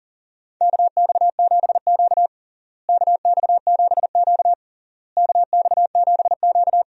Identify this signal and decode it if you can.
Morse code, heard as a keyed tone, about 40 wpm, 700 Hz. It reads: KX7Q KX7Q KX7Q
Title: USA Call Signs - 4 Characters - Repeated 3x - 40wpm